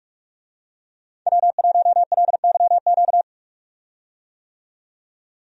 Morse code, heard as a keyed tone, about 43 wpm, 700 Hz. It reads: W1LYQ
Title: USA Call Signs - 5 Characters - 45wpm